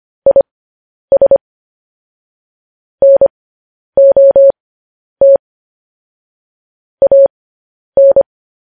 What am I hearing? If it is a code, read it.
Morse code, heard as a keyed tone, about 25 wpm, spaced as for 10 wpm, 550 Hz. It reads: IS NOT AN
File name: podcast / News headlines-1x-25x10